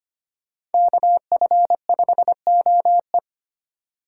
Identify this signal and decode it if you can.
Morse code, heard as a keyed tone, about 32 wpm, 700 Hz. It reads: KF5OE